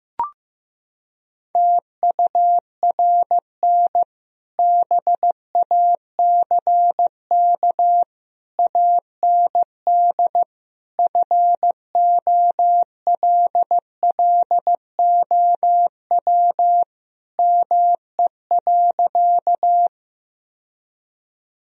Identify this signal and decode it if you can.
Morse code, heard as a keyed tone, about 15 wpm, 700 Hz. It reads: TURN BACK AND FOLLOW ME.
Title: Sentences from Top 1000 Words - Review Part 1 - 15wpm